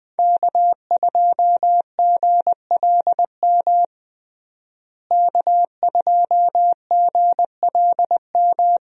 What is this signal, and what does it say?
Morse code, heard as a keyed tone, about 20 wpm, 700 Hz. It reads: K2GLM K2GLM
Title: USA Call Signs - 5 Characters - Repeated 3x - 20wpm